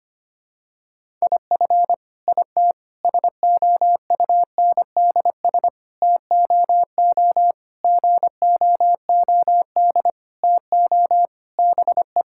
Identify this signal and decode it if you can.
Morse code, heard as a keyed tone, about 25 wpm, 700 Hz. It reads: IF IT SOUNDS TOO GOOD TO BE
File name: practice / 25wpm / Sentences with 9-Word Phrases - 25wpm